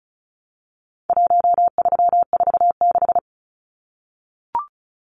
Morse code, heard as a keyed tone, about 34 wpm, 700 Hz. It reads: 1346
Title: Numbers - Contest Style - 4-Digit 35wpm